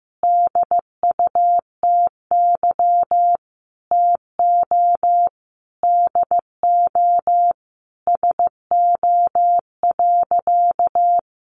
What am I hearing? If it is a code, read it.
Morse code, heard as a keyed tone, about 15 wpm, 700 Hz. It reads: DUTY TO DO SO.
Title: Sentences from Top 1500 Words 15wpm